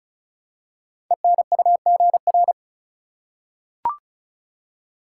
Morse code, heard as a keyed tone, about 36 wpm, 700 Hz. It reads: ENUGR
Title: Code Groups - 5 Letters - 35wpm